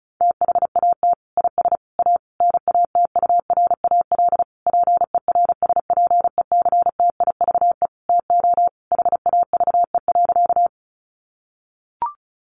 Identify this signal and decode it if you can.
Morse code, heard as a keyed tone, about 34 wpm, 700 Hz. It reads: THAT IS A NATURAL PERSPECTIVE TO HAVE.
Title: Sentences from Top 1200 Words 35wpm